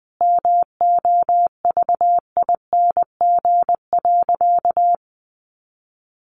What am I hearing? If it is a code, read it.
Morse code, heard as a keyed tone, about 20 wpm, 700 Hz. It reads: MOVING.
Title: A Sound Of Thunder 20wpm